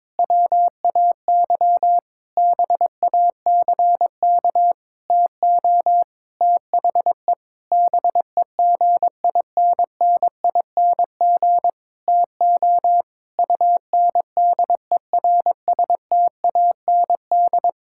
Morse code, heard as a keyed tone, about 21 wpm, 700 Hz. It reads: WAY BACK TO THE BEGINNING TO UNDERSTAND
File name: practice / 22wpm / Sentences with 8-Word Phrases - 22wpm